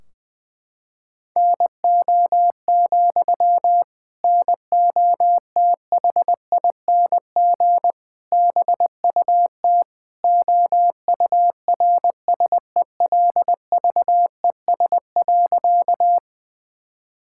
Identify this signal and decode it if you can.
Morse code, heard as a keyed tone, about 20 wpm, 700 Hz. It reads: NO, NOTHING BUT OURSELVES.
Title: Sentences from Top 1600 Words 20wpm